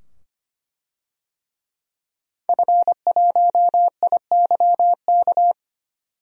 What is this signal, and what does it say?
Morse code, heard as a keyed tone, about 25 wpm, 700 Hz. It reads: F1IYK
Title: DXCC-100 Call Signs - 5 Characters - 25wpm